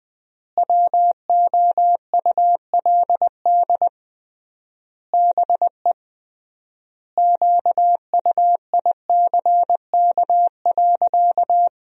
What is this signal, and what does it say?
Morse code, heard as a keyed tone, about 20 wpm, 700 Hz. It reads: WOULD BE QUICK.
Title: 3x Word-Spacing - Sentences from Top 500 Words - 20wpm